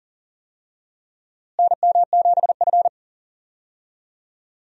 Morse code, heard as a keyed tone, about 40 wpm, 700 Hz. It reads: NM7F
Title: USA Call Signs - 4 Characters - Repeated 3x - 40wpm